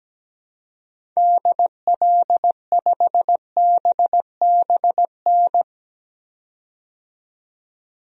Morse code, heard as a keyed tone, about 17 wpm, 700 Hz. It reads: DL5BBN
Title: DXCC-100 Call Signs - 6 Characters - 17wpm